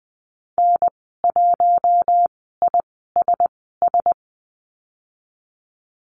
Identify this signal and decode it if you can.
Morse code, heard as a keyed tone, about 20 wpm, spaced as for 14 wpm, 700 Hz. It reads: N1ISS
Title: 042 Lesson 8g - Number 1 - Call Signs 20wpm